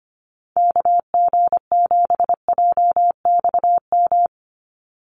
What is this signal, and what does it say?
Morse code, heard as a keyed tone, about 25 wpm, 700 Hz. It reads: KG7JXM
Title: USA Call Signs 25wpm